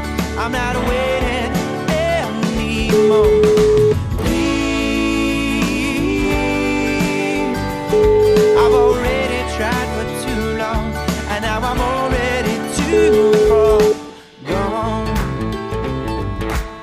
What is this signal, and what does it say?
Ringback tone (Europe, 425 Hz), with noise.